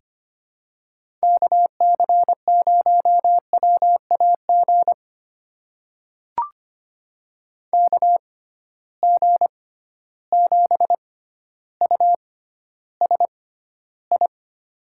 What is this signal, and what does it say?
Morse code, heard as a keyed tone, about 25 wpm, spaced as for 13 wpm, 700 Hz. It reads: KC0WAG KG7USI
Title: USA Call Signs - 6 Characters - 6x Character Spacing - 25wpm